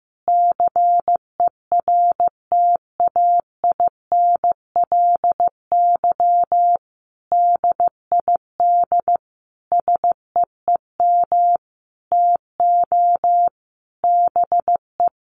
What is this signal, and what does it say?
Morse code, heard as a keyed tone, about 15 wpm, 700 Hz. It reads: CERTAINLY DID SEEM TO BE
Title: Sentences from Top 1100 Words 15wpm